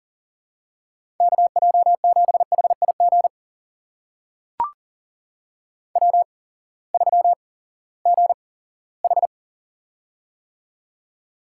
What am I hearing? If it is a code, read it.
Morse code, heard as a keyed tone, about 39 wpm, spaced as for 20 wpm, 700 Hz. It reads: KJ7HIG W3GH
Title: USA Call Signs - 8x Character Spacing - 40wpm